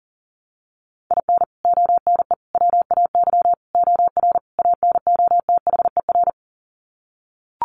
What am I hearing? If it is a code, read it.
Morse code, heard as a keyed tone, about 40 wpm, 700 Hz. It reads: IN ONE WAY OR ANOTHER
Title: Sets of 5 Words Encore 40wpm